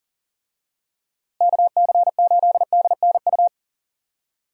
Morse code, heard as a keyed tone, about 40 wpm, 700 Hz. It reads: KC8DNU
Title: USA Call Signs - 6 Characters - Repeated 3x - 40wpm